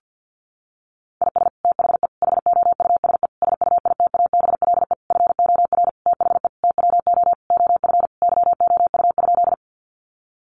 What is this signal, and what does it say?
Morse code, heard as a keyed tone, about 50 wpm, 700 Hz. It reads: IS THE HOUSE SUITABLE FOR THE TWO OF YOU?
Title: Sentences from Top 1400 Words 50wpm